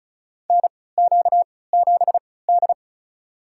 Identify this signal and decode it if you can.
Morse code, heard as a keyed tone, about 36 wpm, spaced as for 20 wpm, 700 Hz. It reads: NQ7D
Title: USA Call Signs - 4 Characters - 3x Character Spacing - 35wpm